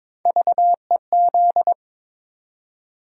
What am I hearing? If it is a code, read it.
Morse code, heard as a keyed tone, about 22 wpm, 700 Hz. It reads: VEZ
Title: USA Call Sign Postfixes - 3 Letters 22wpm